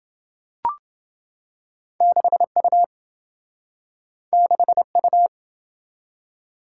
Morse code, heard as a keyed tone, about 26 wpm, 700 Hz. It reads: 6U 6U